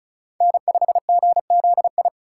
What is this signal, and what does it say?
Morse code, heard as a keyed tone, about 36 wpm, 700 Hz. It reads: N5GZI
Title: USA Call Signs - 5 Characters - 2x Character Spacing - 35wpm